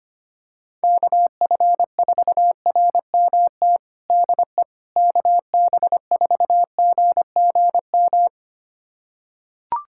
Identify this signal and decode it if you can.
Morse code, heard as a keyed tone, about 25 wpm, 700 Hz. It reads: KF4RMT DE KB4GGM
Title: USA Callsign DE Callsign - 6 Characters - 25wpm